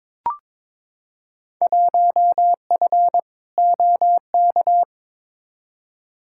Morse code, heard as a keyed tone, about 22 wpm, 700 Hz. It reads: 1F OK
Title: Field Day Exchanges Encore - 22wpm